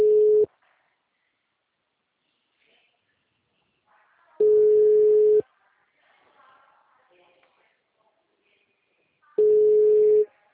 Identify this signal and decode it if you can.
Ringback tone (Europe, 425 Hz).